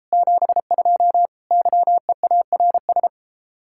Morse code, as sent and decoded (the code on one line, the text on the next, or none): --... ..--- / -.-- . .- .-. ...
72 YEARS